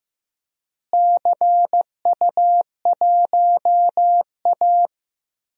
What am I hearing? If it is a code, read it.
Morse code, heard as a keyed tone, about 15 wpm, 700 Hz. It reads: CU1A
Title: DXCC-100 Call Signs - 4 Characters - Repeated 2x - 15wpm